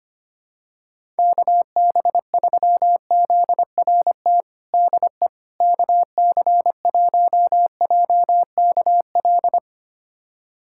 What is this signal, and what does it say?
Morse code, heard as a keyed tone, about 25 wpm, 700 Hz. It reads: KB3ZRT DE KC1JKL
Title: USA Callsign DE Callsign 25wpm